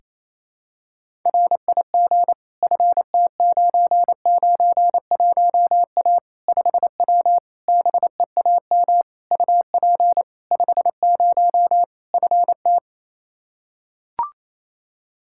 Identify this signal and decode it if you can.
Morse code, heard as a keyed tone, about 28 wpm, 700 Hz. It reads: RIG FT991A 5W BEAM UP 50 FT
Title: Practice QSO - Exchange 2 - Info - 28wpm